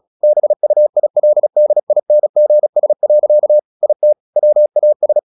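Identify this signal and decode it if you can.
Morse code, heard as a keyed tone, about 36 wpm, 600 Hz. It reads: BUILDINGS. IT WAS
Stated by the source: Morse practice at 36 wpm, text of Ken Follett's novel Never, from CW med Gnister.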